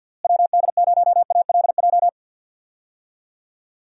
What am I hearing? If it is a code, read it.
Morse code, heard as a keyed tone, about 52 wpm, 700 Hz. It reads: WD0ALJ
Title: USA Call Signs - 6 Characters - Repeated 3x - 50wpm